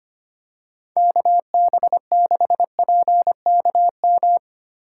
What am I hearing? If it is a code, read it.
Morse code, heard as a keyed tone, about 25 wpm, 700 Hz. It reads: KB6PKM
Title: USA Call Signs - 6 Characters - 8x Character Spacing - 25wpm